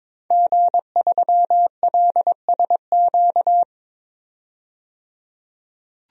Morse code, heard as a keyed tone, about 22 wpm, 700 Hz. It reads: G3LSQ
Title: DXCC-100 Call Signs - 5 Characters - 22wpm